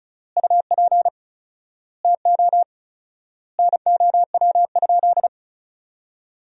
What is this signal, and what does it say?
Morse code, heard as a keyed tone, about 36 wpm, 700 Hz. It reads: UP TO NOW?
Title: 4x Word-Spacing - Sentences from Top 100 Words - 35wpm